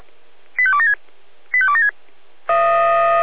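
Five-tone selective call (ZVEI), heard as quick runs of five tones, tones of about 70 ms. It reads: 85276 85276
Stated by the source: ZVEI five-tone selective call received by SDR.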